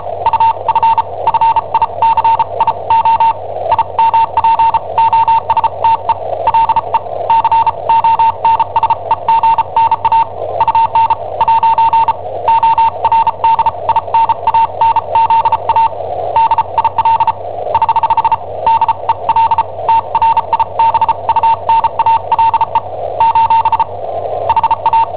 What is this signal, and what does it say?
Morse code, heard as a keyed tone, about 34 wpm, spaced as for 28 wpm, 910 Hz. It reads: UFFICIO IMPOSTE LE CONSEGNA UN'ORDINANZA DIL <HH> DEL TRIBUNALE : HA
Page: Cassetta esercizi preparata da I7OHP per IW5ELP